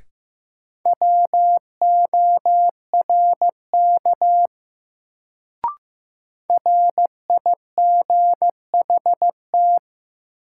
Morse code, heard as a keyed tone, about 15 wpm, 700 Hz. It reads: WORK RIGHT
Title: Sets of 2 Words - New Word Drill - 15wpm